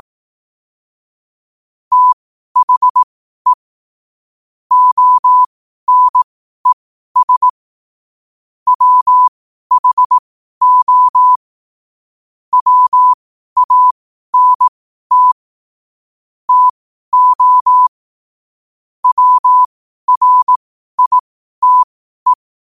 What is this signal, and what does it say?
Morse code, heard as a keyed tone, about 18 wpm, spaced as for 12 wpm, 1000 Hz. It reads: THE ONES WHO WANT TO WRITE
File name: qotdfiles / 12wpm